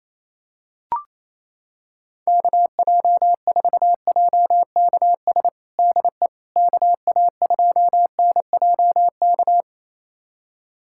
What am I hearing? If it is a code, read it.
Morse code, heard as a keyed tone, about 28 wpm, 700 Hz. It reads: KJ4JKS DE KA2NJK
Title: USA Callsign DE Callsign - 6 Characters - 28wpm